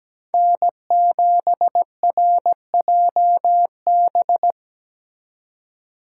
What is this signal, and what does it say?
Morse code, heard as a keyed tone, about 20 wpm, 700 Hz. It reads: N7RJB
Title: Speed Racing - USA 1x3 Call Signs 17wpm